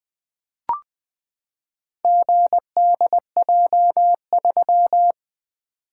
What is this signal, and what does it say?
Morse code, heard as a keyed tone, about 20 wpm, 700 Hz. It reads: GDJ3
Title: Four Letter-Number Combinations - 20wpm